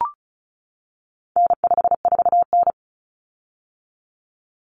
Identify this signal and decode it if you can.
Morse code, heard as a keyed tone, about 34 wpm, 700 Hz. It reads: N54N
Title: Common Cut Numbers - 4-Digit 35wpm